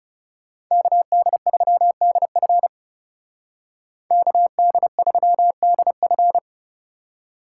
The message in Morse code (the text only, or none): KD3DF KD3DF